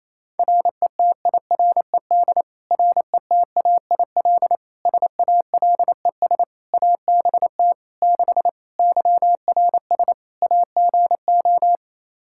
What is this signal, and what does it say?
Morse code, heard as a keyed tone, about 28 wpm, 700 Hz. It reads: RETIRED RETAIL SALES ABT 6 YRS AGO
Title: Practice QSO - Exchange 3 - More Info - 28wpm